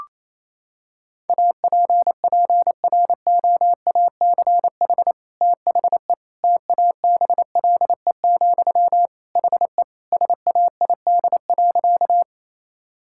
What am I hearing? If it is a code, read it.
Morse code, heard as a keyed tone, about 28 wpm, 700 Hz. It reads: APPROACH THE TABLE, HE SAID.
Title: Sentences from Top 1000 Words - Review Part 6 - 28wpm